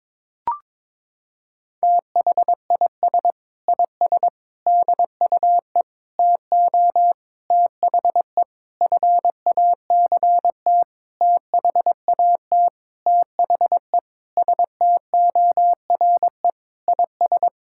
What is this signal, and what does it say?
Morse code, heard as a keyed tone, about 22 wpm, 700 Hz. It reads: THIS IS DUE TO THE FACT THAT THE STORE IS
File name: Sentences with 7-Word Phrases - 22wpm